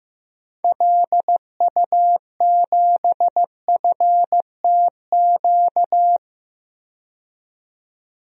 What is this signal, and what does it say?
Morse code, heard as a keyed tone, about 15 wpm, 700 Hz. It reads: LU7FTQ
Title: DXCC-100 Call Signs - 6 Characters - 15wpm